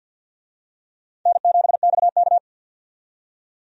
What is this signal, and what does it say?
Morse code, heard as a keyed tone, about 53 wpm, 700 Hz. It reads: N7XK
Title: USA Call Signs - 4 Characters - 50wpm